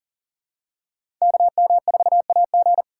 Morse code, heard as a keyed tone, about 38 wpm, 700 Hz. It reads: KM4AG
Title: USA Call Signs - 5 Characters - 6x Character Spacing - 40wpm